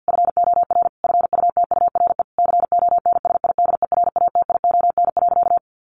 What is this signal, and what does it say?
Morse code, heard as a keyed tone, about 50 wpm, 700 Hz. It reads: FOR FUTURE CONSIDERATION.
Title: Sentences from Top 1500 Words 50wpm